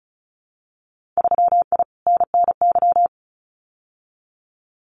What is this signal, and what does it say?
Morse code, heard as a keyed tone, about 34 wpm, 700 Hz. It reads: 3I NNY
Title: Winter Field Day Exchanges 35wpm